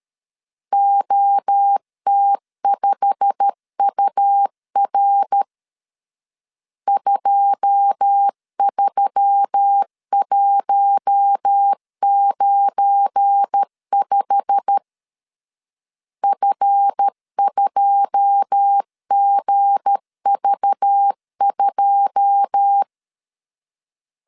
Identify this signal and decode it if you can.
Morse code, heard as a keyed tone, about 13 wpm, 800 Hz. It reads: OT5UR 23195 F2GV2
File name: LEZIONI / 9 2 12wpm 2301